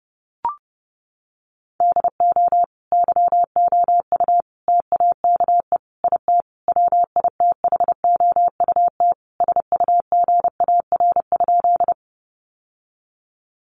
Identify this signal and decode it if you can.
Morse code, heard as a keyed tone, about 30 wpm, 700 Hz. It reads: DO YOU TAKE IT WITHOUT SUGAR?
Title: Sentences from Top 1200 Words 30wpm